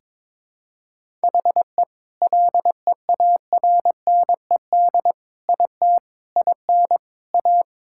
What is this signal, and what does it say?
Morse code, heard as a keyed tone, about 22 wpm, 700 Hz. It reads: HE LEARNED IT IN A
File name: Sentences with 7-Word Phrases - 22wpm